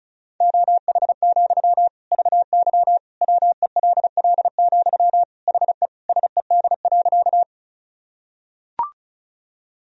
Morse code, heard as a keyed tone, about 34 wpm, 700 Hz. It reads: OH, VY WELL, HE SED.
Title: Sentences from Top 500 Words and Abbreviations 35wpm